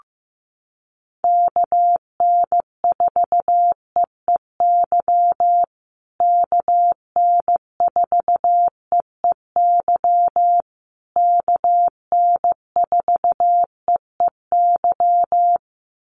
Morse code, heard as a keyed tone, about 15 wpm, 700 Hz. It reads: KN4EEY KN4EEY KN4EEY
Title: USA Call Signs Repeated 3x 15wpm